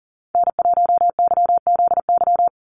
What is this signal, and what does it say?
Morse code, heard as a keyed tone, about 39 wpm, 690 Hz. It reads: N1YZY
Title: USA Call Signs 40wpm